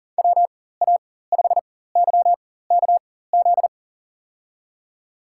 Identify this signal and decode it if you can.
Morse code, heard as a keyed tone, about 40 wpm, spaced as for 19 wpm, 700 Hz. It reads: WA5YKZ
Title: USA Call Signs - 6 Characters - 4x Character Spacing - 40wpm